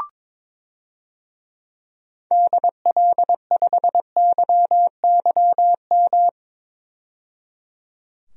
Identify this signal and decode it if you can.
Morse code, heard as a keyed tone, about 21 wpm, 700 Hz. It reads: DL5YYM
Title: DXCC-100 Call Signs - 6 Characters - 22wpm